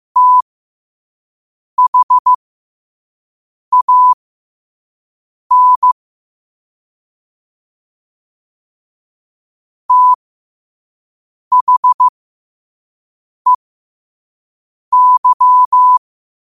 Morse code, heard as a keyed tone, about 15 wpm, spaced as for 5 wpm, 1000 Hz. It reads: THAN THEY
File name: qotdfiles / 5wpm